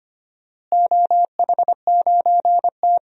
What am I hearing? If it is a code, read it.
Morse code, heard as a keyed tone, about 25 wpm, 700 Hz. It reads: OH9T
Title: DXCC-100 Call Signs - 4 Characters - Repeated 2x - 25wpm